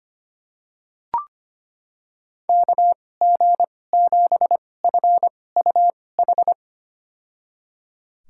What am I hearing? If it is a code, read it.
Morse code, heard as a keyed tone, about 25 wpm, spaced as for 18 wpm, 700 Hz. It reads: KG7FUH
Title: USA Call Signs - 6 Characters - 2x Character Spacing - 25wpm